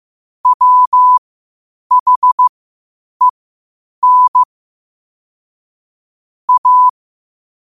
Morse code, heard as a keyed tone, about 15 wpm, spaced as for 8 wpm, 1000 Hz. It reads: WHEN A
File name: qotdfiles / 8wpm